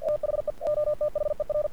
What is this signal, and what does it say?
Morse code, heard as a keyed tone, about 47 wpm, 590 Hz. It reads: THE OTHER
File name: archived / CRSnet